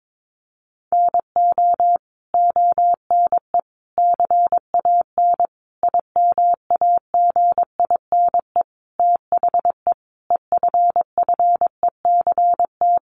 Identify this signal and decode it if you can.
Morse code, heard as a keyed tone, about 22 wpm, 700 Hz. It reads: NO ONE CAN IMAGINE THE EFFECT
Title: Sentences from Top 1400 Words 22wpm